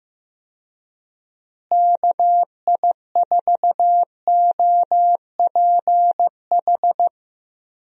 Morse code, heard as a keyed tone, about 15 wpm, 700 Hz. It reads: KI4OPH
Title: USA Call Signs - 6 Characters - Repeated 2x - 15wpm